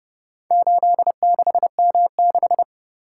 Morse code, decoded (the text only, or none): 86M6